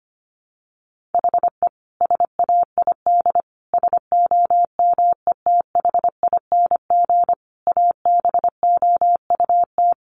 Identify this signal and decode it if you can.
Morse code, heard as a keyed tone, about 25 wpm, 700 Hz. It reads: HE SAID SOMETHING ABOUT
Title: Sentences from Top 1500 Words 25wpm